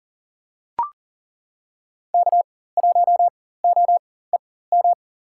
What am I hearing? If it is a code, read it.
Morse code, heard as a keyed tone, about 40 wpm, spaced as for 19 wpm, 700 Hz. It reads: K1OEM
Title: USA Call Signs - 4x Character Spacing - 40wpm